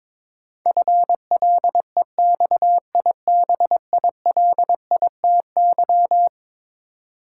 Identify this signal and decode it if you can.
Morse code, heard as a keyed tone, about 22 wpm, 700 Hz. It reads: FLEXIBILITY
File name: practice / 22wpm / Eleven Letter Words - 22wpm